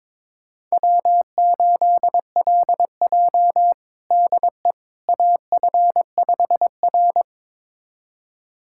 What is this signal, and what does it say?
Morse code, heard as a keyed tone, about 22 wpm, 700 Hz. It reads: W8LJ DE AF5R
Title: USA Callsign DE Callsign - 4 Characters - 22wpm